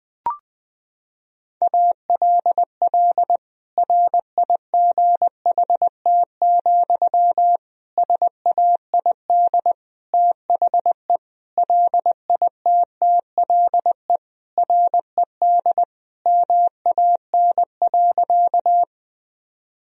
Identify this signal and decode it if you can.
Morse code, heard as a keyed tone, about 20 wpm, 700 Hz. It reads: ALL RIGHT, SAID THE LITTLE RED MAN.
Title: Sentences from Top 1000 Words - Review Part 2 - 20wpm